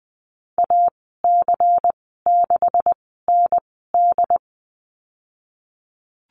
Morse code, heard as a keyed tone, about 20 wpm, spaced as for 15 wpm, 700 Hz. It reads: AC6ND
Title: 179 Lesson 29g - Slash - Call Signs 20wpm